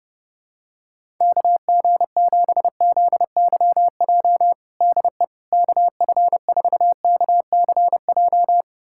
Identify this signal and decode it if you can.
Morse code, heard as a keyed tone, about 32 wpm, 700 Hz. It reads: KG7ZYJ DE KF4KCJ
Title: USA Callsign DE Callsign - 6 Characters - 30wpm